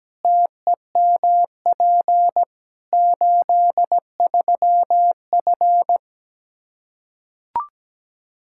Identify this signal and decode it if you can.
Morse code, heard as a keyed tone, about 17 wpm, 700 Hz. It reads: TEMP 83F
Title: K1USN SST QSO Practice 17wpm